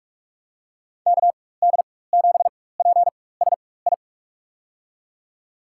Morse code, heard as a keyed tone, about 47 wpm, spaced as for 21 wpm, 700 Hz. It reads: KD7PSI
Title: USA Call Signs - 4x Character Spacing - 45wpm